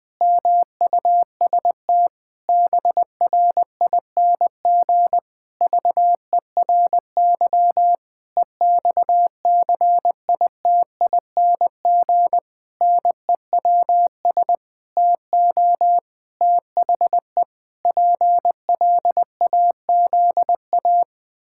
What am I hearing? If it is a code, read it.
Morse code, heard as a keyed tone, about 20 wpm, 700 Hz. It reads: MUST BRING VERY EXCITING NEWS TO THE PLAZA
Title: Pangram Challenge v2 - 20wpm